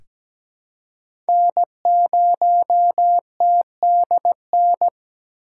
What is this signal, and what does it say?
Morse code, heard as a keyed tone, about 17 wpm, 700 Hz. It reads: N0TDN
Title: USA Call Signs - 2x Character Spacing - 17wpm